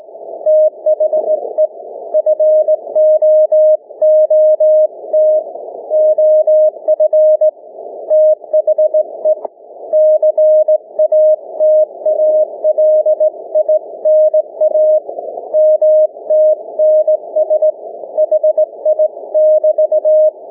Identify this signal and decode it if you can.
Morse code, heard as a keyed tone, about 19 wpm, 620 Hz. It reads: THE FOOT OF THE CATALINA MTNS HI<BT>
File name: cw / ssn 30M